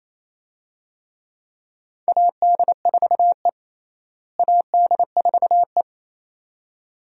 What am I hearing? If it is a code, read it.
Morse code, heard as a keyed tone, about 28 wpm, 700 Hz. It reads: AD4E AD4E
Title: USA Call Signs - 4 Characters - Repeated 2x - 28wpm